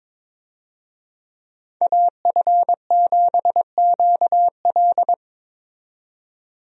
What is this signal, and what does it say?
Morse code, heard as a keyed tone, about 22 wpm, 700 Hz. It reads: AF7QL
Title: USA Call Signs - 5 Characters - 22wpm